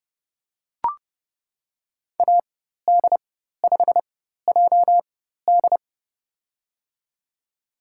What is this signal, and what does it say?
Morse code, heard as a keyed tone, about 30 wpm, spaced as for 14 wpm, 700 Hz. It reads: AD5JD
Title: USA Call Signs - 5 Characters - 4x Character Spacing - 30wpm